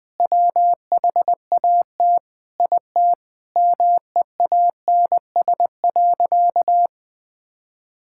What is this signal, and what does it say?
Morse code, heard as a keyed tone, about 20 wpm, 700 Hz. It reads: WHAT IT MEANS.
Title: Sentences with 5-Word Phrases - 20wpm